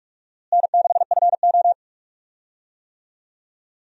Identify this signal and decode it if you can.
Morse code, heard as a keyed tone, about 44 wpm, 700 Hz. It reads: N6FO
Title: USA Call Signs - 4 Characters - Repeated 2x - 45wpm